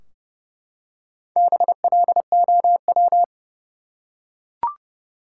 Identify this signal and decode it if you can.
Morse code, heard as a keyed tone, about 31 wpm, 700 Hz. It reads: BLOW
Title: Four Letter Words - 30wpm